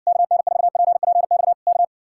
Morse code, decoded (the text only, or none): KN4PPX K